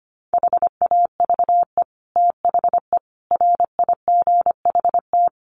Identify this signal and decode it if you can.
Morse code, heard as a keyed tone, about 25 wpm, 700 Hz. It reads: HAVE THE RIGHT
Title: Sets of 4 Words 25wpm